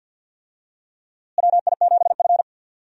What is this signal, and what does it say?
Morse code, heard as a keyed tone, about 52 wpm, 700 Hz. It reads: WI7F
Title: USA Call Signs - 4 Characters - Repeated 3x - 50wpm